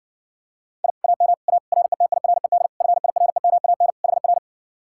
Morse code, heard as a keyed tone, about 69 wpm, 700 Hz. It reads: I AM A RETIRED FIREMAN HR